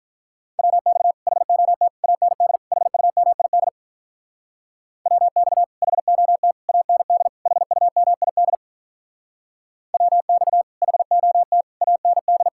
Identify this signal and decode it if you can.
Morse code, heard as a keyed tone, about 45 wpm, 700 Hz. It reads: WX HOT AND HUMID WX HOT AND HUMID WX HOT AND